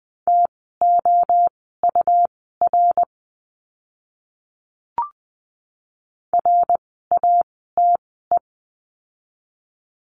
Morse code, heard as a keyed tone, about 20 wpm, spaced as for 14 wpm, 700 Hz. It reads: TOUR RATE
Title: 101 Lesson 17f - Letter u - Words 20wpm